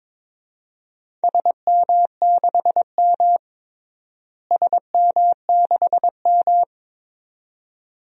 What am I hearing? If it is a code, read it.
Morse code, heard as a keyed tone, about 22 wpm, 700 Hz. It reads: SM6M SM6M
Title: DXCC-100 Call Signs - 4 Characters - Repeated 2x - 22wpm